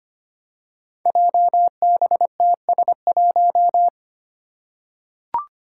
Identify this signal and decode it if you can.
Morse code, heard as a keyed tone, about 25 wpm, 700 Hz. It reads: JBTS1
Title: Five Challenging Letter-Number Combinations 25wpm